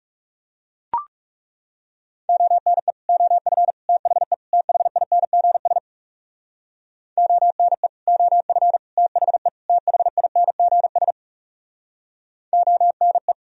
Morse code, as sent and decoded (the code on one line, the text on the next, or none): --- -. / --- ..-. / - .... / - .... .. -. --. ... / --- -. / --- ..-. / - .... / - .... .. -. --. ... / --- -. .
ONE OF THE THINGS ONE OF THE THINGS ONE